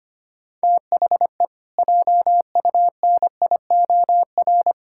Morse code, heard as a keyed tone, about 25 wpm, 700 Hz. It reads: THE JUNIOR
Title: Pangram Challenge v2 - 25wpm